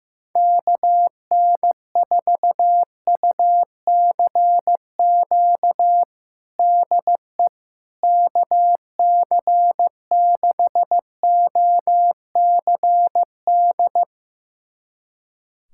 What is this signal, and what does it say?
Morse code, heard as a keyed tone, about 15 wpm, 700 Hz. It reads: KN4UCQ DE KC6OCD
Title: USA Callsign DE Callsign - 6 Characters - 15wpm